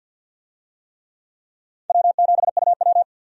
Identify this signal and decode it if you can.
Morse code, heard as a keyed tone, about 52 wpm, 700 Hz. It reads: W7UW
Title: USA Call Signs - 4 Characters - 50wpm